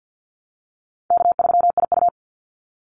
Morse code, heard as a keyed tone, about 49 wpm, 700 Hz. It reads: K3IU
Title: Easy Call Signs Repeated 3x - 50wpm